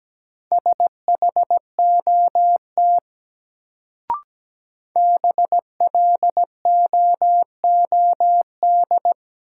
Morse code, heard as a keyed tone, about 17 wpm, 700 Hz. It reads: SHOT BLOOD
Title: Sentences from Top 1100 Words - New Word Drill Alt - 17wpm